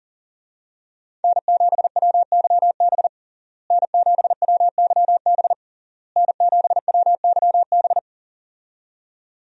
Morse code, heard as a keyed tone, about 37 wpm, 700 Hz. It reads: N7WYB N7WYB N7WYB
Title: USA Call Signs - 5 Characters - Repeated 3x - 40wpm